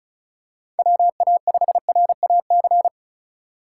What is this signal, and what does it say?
Morse code, heard as a keyed tone, about 34 wpm, 700 Hz. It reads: WA5RAC
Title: USA Call Signs - 2x Character Spacing - 35wpm